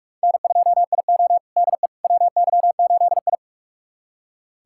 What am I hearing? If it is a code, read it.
Morse code, heard as a keyed tone, about 44 wpm, 700 Hz. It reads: N2IO DE WY8I
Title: USA Callsign DE Callsign - 4 Characters - 45wpm